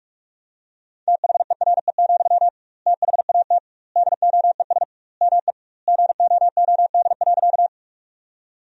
Morse code, heard as a keyed tone, about 46 wpm, 700 Hz. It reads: THERE, THAT DOES ME GOOD.
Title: 2x Word-Spacing - Sentences from Top 200 Words - 45wpm